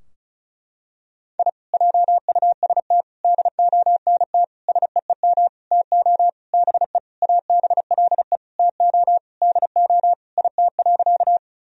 Morse code, heard as a keyed tone, about 34 wpm, 700 Hz. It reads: I JUST DONT SEEM TO BE ABLE TO DO IT.
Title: Sentences from Top 1000 Words - Review Part 2 - 35wpm